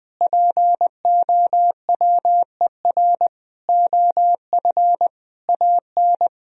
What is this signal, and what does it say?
Morse code, heard as a keyed tone, about 20 wpm, 700 Hz. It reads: POWER OF AN
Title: Sentences from Top 1000 Words - Review Part 4 - 20wpm